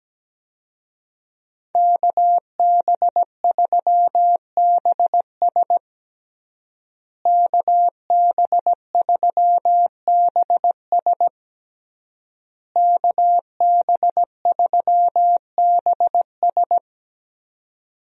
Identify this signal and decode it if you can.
Morse code, heard as a keyed tone, about 17 wpm, 700 Hz. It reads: KB3BS KB3BS KB3BS
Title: USA Call Signs - 5 Characters - Repeated 3x - 17wpm